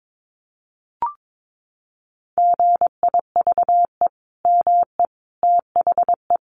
Morse code, heard as a keyed tone, about 22 wpm, 700 Hz. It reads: GIVE ME THE
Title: Sentences from Top 1400 Words 22wpm